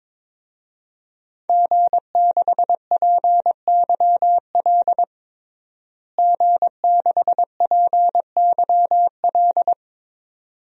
Morse code, heard as a keyed tone, about 22 wpm, 700 Hz. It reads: G6PYL G6PYL
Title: DXCC-100 Call Signs - 5 Characters - Repeated 2x - 22wpm